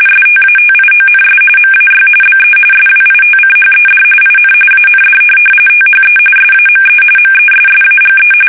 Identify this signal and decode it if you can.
RTTY, about 45.45 baud. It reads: STATIONS MAY BE SWITCHED IN AND THE SAME MESSAGE
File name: fsk850